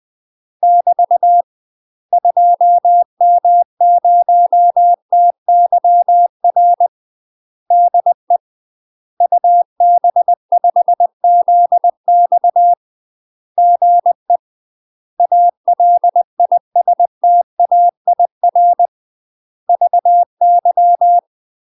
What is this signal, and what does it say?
Morse code, heard as a keyed tone, about 20 wpm, 700 Hz. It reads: <BT> 2M0TYR DE UB5ZX GE ALISTAIR VY
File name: UK MASTER QSOs TRAINING / 20WPM / UK MASTER QSOs 0004